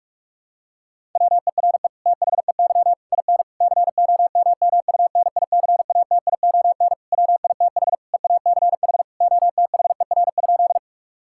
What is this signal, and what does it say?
Morse code, heard as a keyed tone, about 46 wpm, 700 Hz. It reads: WERE THEY IN COMMUNICATION WITH EACH OTHER?
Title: Sentences from Top 1000 Words - Review Part 6 - 45wpm